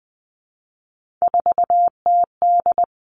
Morse code, heard as a keyed tone, about 19 wpm, 700 Hz. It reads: 4TD
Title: Three Challenging Letter-Number Combinations 20wpm